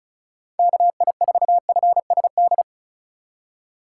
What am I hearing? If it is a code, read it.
Morse code, heard as a keyed tone, about 34 wpm, 700 Hz. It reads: KI4FSD
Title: USA Call Signs - 6 Characters - 3x Character Spacing - 35wpm